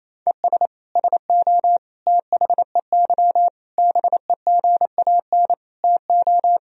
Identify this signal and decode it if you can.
Morse code, heard as a keyed tone, about 28 wpm, 700 Hz. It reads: ES SO THEY BEGAN TO
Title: Sentences from Top 400 Words and Abbreviations 28wpm